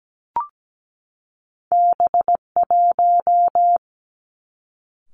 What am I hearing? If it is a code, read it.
Morse code, heard as a keyed tone, about 17 wpm, 700 Hz. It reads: B1
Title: Two Challenging Letter-Number Combinations 17wpm